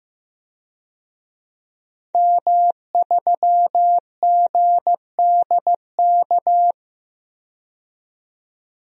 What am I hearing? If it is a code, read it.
Morse code, heard as a keyed tone, about 15 wpm, 700 Hz. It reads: M3GDK
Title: DXCC-100 Call Signs - 5 Characters - 15wpm